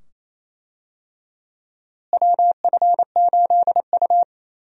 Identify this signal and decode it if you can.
Morse code, heard as a keyed tone, about 28 wpm, 700 Hz. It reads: WF8U
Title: USA Call Signs - 4 Characters - Repeated 2x - 28wpm